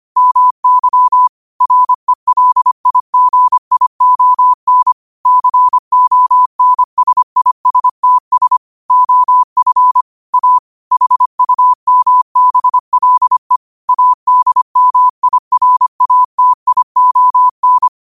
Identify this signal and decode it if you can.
Morse code, heard as a keyed tone, about 25 wpm, 1000 Hz. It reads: MY RELIGION CONSISTS OF A HUMBLE ADMIRATION
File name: qotdfiles / 25wpm